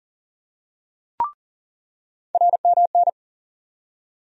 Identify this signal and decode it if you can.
Morse code, heard as a keyed tone, about 40 wpm, 700 Hz. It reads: RMN
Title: USA Call Sign Postfixes - 3 Letters 40wpm